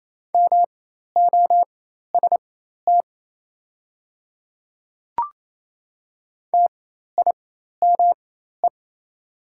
Morse code, heard as a keyed tone, about 28 wpm, spaced as for 13 wpm, 700 Hz. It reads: MOST TIME
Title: 4x Character Spacing - Top 300 Words - 28wpm